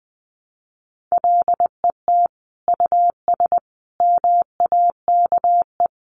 Morse code, heard as a keyed tone, about 20 wpm, 700 Hz. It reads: LET US MAKE
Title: Sentences from Top 700 Words 20wpm